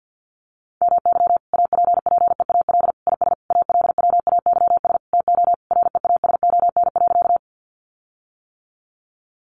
Morse code, heard as a keyed tone, about 50 wpm, 700 Hz. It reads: MY APPEAL IS ALWAYS TO REASON.
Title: Sentences from Top 1300 Words 50wpm